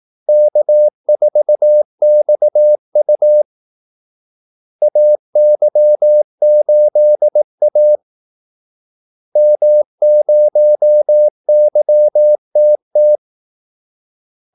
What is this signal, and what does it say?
Morse code, heard as a keyed tone, about 18 wpm, 600 Hz. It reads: K4XU AY8A M0YTT
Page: Kaldesignaler 18 wpm